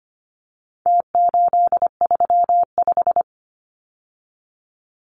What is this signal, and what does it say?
Morse code, heard as a keyed tone, about 25 wpm, 700 Hz. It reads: T835
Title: Numbers - Contest Style - 4-Digit 25wpm